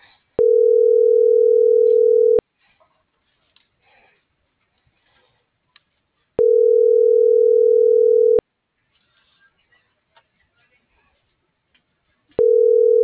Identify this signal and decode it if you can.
Ringback tone (North America), 440 + 480 Hz.